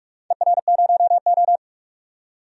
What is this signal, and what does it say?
Morse code, heard as a keyed tone, about 46 wpm, 700 Hz. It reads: ER0O
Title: DXCC-100 Call Signs - 4 Characters - 45wpm